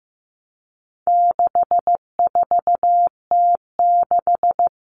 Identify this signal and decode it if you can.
Morse code, heard as a keyed tone, about 15 wpm, 700 Hz. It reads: 64T6
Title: Common Cut Numbers - 4-Digit 15wpm